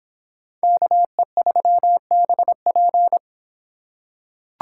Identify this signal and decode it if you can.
Morse code, heard as a keyed tone, about 26 wpm, 700 Hz. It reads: KE3BP